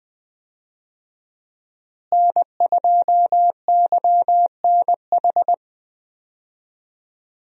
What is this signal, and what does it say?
Morse code, heard as a keyed tone, about 20 wpm, 700 Hz. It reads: N2YNH
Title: USA Call Signs - 5 Characters - Repeated 3x - 20wpm